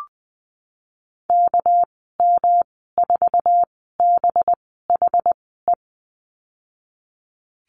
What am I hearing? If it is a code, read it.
Morse code, heard as a keyed tone, about 20 wpm, spaced as for 15 wpm, 700 Hz. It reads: KM4BHE
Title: 200 Lesson 32g - Letter k - Call Signs 20wpm